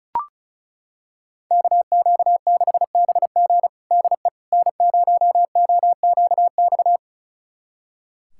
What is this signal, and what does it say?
Morse code, heard as a keyed tone, about 36 wpm, 700 Hz. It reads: KQ6BG DE N0OQX
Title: USA Callsign DE Callsign - 5 Characters - 35wpm